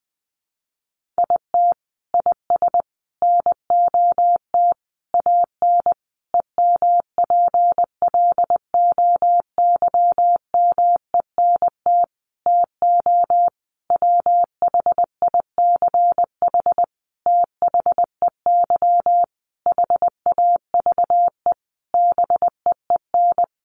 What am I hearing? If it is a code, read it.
Morse code, heard as a keyed tone, about 20 wpm, 700 Hz. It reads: IT IS NOT AN EMPLOYMENT TO WHICH THEY HAVE BEEN
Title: Sentences from Top 1500 Words 20wpm